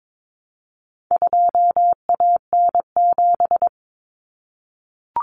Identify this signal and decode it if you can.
Morse code, heard as a keyed tone, about 22 wpm, 700 Hz. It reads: 2AN7
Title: Common Cut Numbers - 4-Digit 22wpm